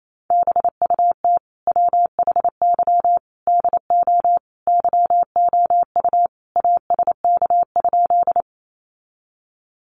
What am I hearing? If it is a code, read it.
Morse code, heard as a keyed tone, about 28 wpm, 700 Hz. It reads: BUT WHY DO YOU ASK?
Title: Sentences from Top 200 Words 28wpm